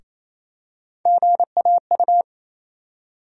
Morse code, heard as a keyed tone, about 28 wpm, 700 Hz. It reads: GAU
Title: USA Call Sign Postfixes - 3 Letters 28wpm